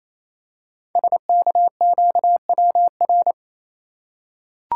Morse code, heard as a keyed tone, about 27 wpm, 700 Hz. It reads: SKQWR
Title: Code Groups - 5 Letter:Numbers - 28wpm